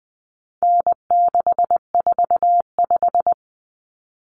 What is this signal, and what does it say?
Morse code, heard as a keyed tone, about 20 wpm, 700 Hz. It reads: N645
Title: Common Cut Numbers - 4-Digit 20wpm